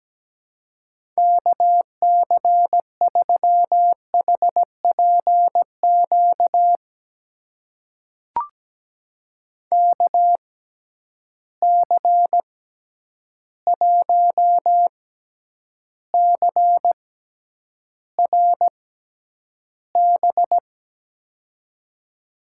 Morse code, heard as a keyed tone, about 17 wpm, spaced as for 9 wpm, 700 Hz. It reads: KC3HPQ KC1CRB
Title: USA Call Signs - 6 Characters - 6x Character Spacing - 17wpm